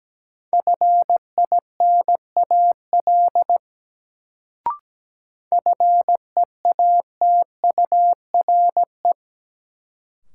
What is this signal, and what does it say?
Morse code, heard as a keyed tone, about 17 wpm, 700 Hz. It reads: FINAL FEATURE
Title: Sentences from Top 800 Words - New Word Drill - 17wpm